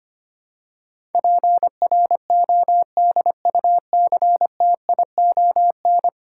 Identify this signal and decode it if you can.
Morse code, heard as a keyed tone, about 25 wpm, 700 Hz. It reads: PRODUCTION
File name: Ten Letter Words - 25wpm